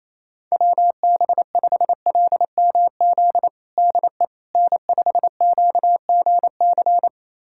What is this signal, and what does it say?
Morse code, heard as a keyed tone, about 28 wpm, 700 Hz. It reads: WB5LMZ DE N5QGC
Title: USA Callsign DE Callsign 28wpm